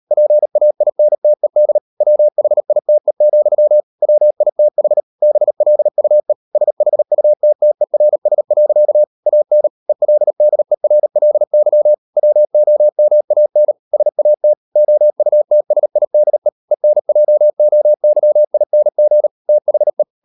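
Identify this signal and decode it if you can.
Morse code, heard as a keyed tone, about 38 wpm, 600 Hz. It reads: PAINTED WHITE, WITH BLUE SHUTTERS. AN ELDERLY WOMAN SAT OUTSIDE ENJOYING THE
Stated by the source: Morse code practice at 38 WPM, sending Ken Follett's novel Never in English, from CW med Gnister.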